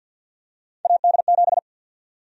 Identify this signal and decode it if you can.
Morse code, heard as a keyed tone, about 48 wpm, 700 Hz. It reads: AD7
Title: USA Call Sign Prefixes - 50wpm